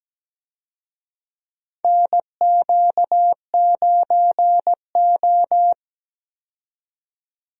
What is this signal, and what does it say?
Morse code, heard as a keyed tone, about 17 wpm, 700 Hz. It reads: NQ9O
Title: USA Call Signs - 4 Characters - 17wpm